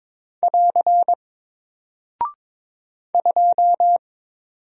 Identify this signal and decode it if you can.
Morse code, heard as a keyed tone, about 22 wpm, 700 Hz. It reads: <AR> 2
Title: Single Character (Letter:Number:Punctuation:Prosign) Alt 22wpm